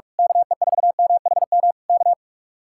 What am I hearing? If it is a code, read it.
Morse code, heard as a keyed tone, about 45 wpm, 700 Hz. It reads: KE4MHM K
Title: Speed-Racing Common QSO Phrases Encore - 45wpm